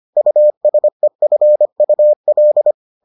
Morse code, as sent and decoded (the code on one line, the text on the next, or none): ..- ... ..-. ..- .-..
USEFUL